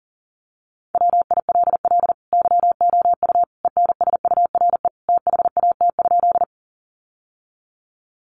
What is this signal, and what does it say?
Morse code, heard as a keyed tone, about 40 wpm, 700 Hz. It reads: WILL YOU ENSURE THAT?
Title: Sentences from Top 1100 Words 40wpm